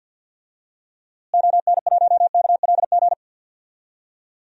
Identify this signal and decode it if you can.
Morse code, heard as a keyed tone, about 52 wpm, 700 Hz. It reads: ON1KLG
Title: DXCC-100 Call Signs - 6 Characters - Repeated 3x - 50wpm